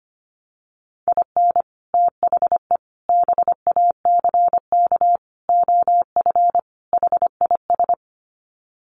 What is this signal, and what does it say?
Morse code, heard as a keyed tone, about 25 wpm, 700 Hz. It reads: IN THE BACK OF HIS
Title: Sets of 5 Words Encore2 25wpm